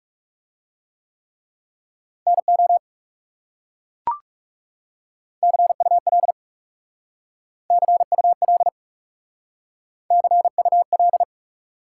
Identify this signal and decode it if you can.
Morse code, heard as a keyed tone, about 36 wpm, 700 Hz. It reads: NO CUL CUL CUL
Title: Speed-Racing Top 180 QSO Elements 30wpm